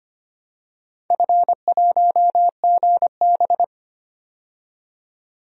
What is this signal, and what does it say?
Morse code, heard as a keyed tone, about 25 wpm, 700 Hz. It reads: F1GB
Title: DXCC-100 Call Signs - 4 Characters - 25wpm